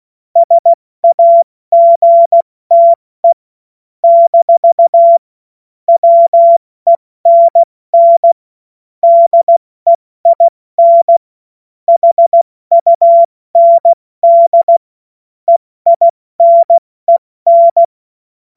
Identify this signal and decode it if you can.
Morse code, heard as a keyed tone, about 16 wpm, spaced as for 14 wpm, 680 Hz. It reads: SAGTE - WENN DEIN HUND EINEN
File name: Morsekurs / Witze / Witz 04. Oktober 2022 70 F80